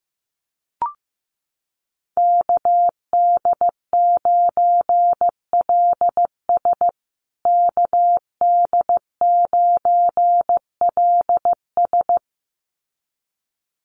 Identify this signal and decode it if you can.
Morse code, heard as a keyed tone, about 15 wpm, 700 Hz. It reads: KD9LS KD9LS
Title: USA Call Signs Repeated 2x 15wpm